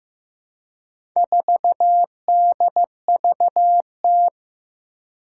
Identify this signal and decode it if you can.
Morse code, heard as a keyed tone, about 15 wpm, 700 Hz. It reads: 4DVT
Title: Four Challenging Letter-Number Combinations 15wpm